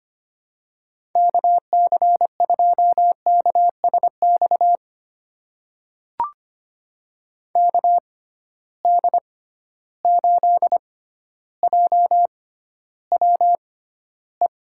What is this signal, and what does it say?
Morse code, heard as a keyed tone, about 25 wpm, spaced as for 13 wpm, 700 Hz. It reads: KC2KSX KD8JWE
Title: USA Call Signs - 6 Characters - 6x Character Spacing - 25wpm